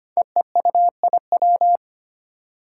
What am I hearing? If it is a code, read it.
Morse code, heard as a keyed tone, about 25 wpm, 700 Hz. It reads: EEUIW
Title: Five Challenging Letter-Number Combinations 25wpm